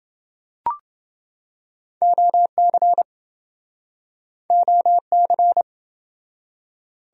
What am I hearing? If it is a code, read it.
Morse code, heard as a keyed tone, about 26 wpm, 700 Hz. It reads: OC OC